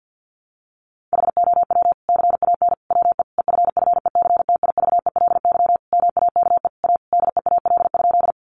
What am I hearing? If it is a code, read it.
Morse code, heard as a keyed tone, about 50 wpm, 700 Hz. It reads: HOW CAN WE EFFECTIVELY MAKE A DEAL?
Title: Sentences from Top 1400 Words 50wpm